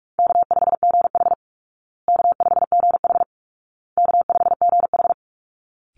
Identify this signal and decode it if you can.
Morse code, heard as a keyed tone, about 44 wpm, 700 Hz. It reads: K5GH K5GH K5GH
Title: Easy Call Signs Repeated 3x - 45wpm